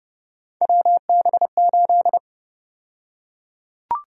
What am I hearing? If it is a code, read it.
Morse code, heard as a keyed tone, about 31 wpm, 700 Hz. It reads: WB8
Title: USA Call Sign Prefixes - Repeated 2x - 30wpm